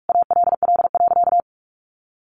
Morse code, heard as a keyed tone, about 44 wpm, 700 Hz. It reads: ALL.
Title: Sentences from Top 1200 Words 45wpm